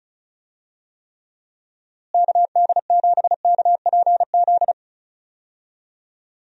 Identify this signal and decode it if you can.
Morse code, heard as a keyed tone, about 36 wpm, 700 Hz. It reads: KD7KPZ
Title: USA Call Signs - 6 Characters - Repeated 2x - 35wpm